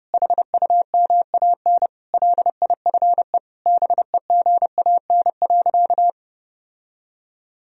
Morse code, heard as a keyed tone, about 32 wpm, 700 Hz. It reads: HUMAN LIFE BEGAN.
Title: Sentences from Top 1000 Words - Review Part 3 - 30wpm